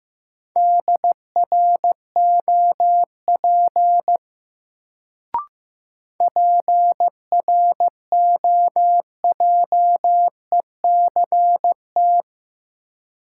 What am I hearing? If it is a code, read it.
Morse code, heard as a keyed tone, about 15 wpm, 700 Hz. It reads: DROP PROJECT
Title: Sentences from Top 1400 Words - New Word Drill Alt - 15wpm